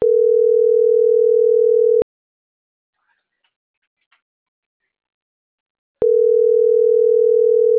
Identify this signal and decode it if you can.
Ringback tone (North America), 440 + 480 Hz.